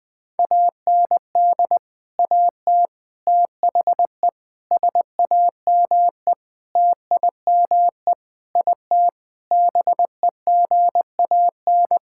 Morse code, heard as a keyed tone, about 21 wpm, 700 Hz. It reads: AND AT THE SAME TIME IT BEGAN
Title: Sentences with 5-Word Phrases - 20wpm